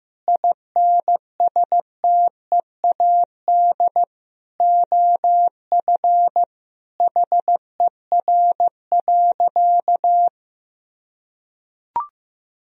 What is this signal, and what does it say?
Morse code, heard as a keyed tone, about 16 wpm, 700 Hz. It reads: INSTEAD OF HER.
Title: Sentences from Top 1000 Words - Review Part 4 - 15wpm